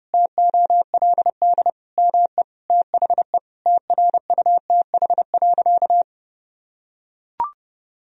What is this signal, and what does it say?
Morse code, heard as a keyed tone, about 30 wpm, 700 Hz. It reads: TOLD ME THE TRUTH.
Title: Sentences from Top 1000 Words - Review Part 4 - 30wpm